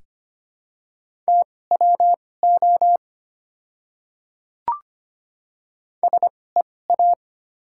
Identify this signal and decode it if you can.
Morse code, heard as a keyed tone, about 25 wpm, spaced as for 18 wpm, 700 Hz. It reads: TWO SEA
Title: 2x Character Spacing - Top 300 Words - 25wpm